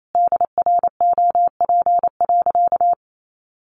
Morse code, heard as a keyed tone, about 28 wpm, 700 Hz. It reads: DROP.
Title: Sentences from Top 1400 Words 28wpm